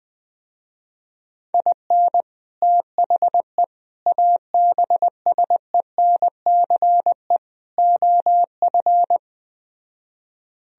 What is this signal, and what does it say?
Morse code, heard as a keyed tone, about 20 wpm, 700 Hz. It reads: IN THE ABSENCE OF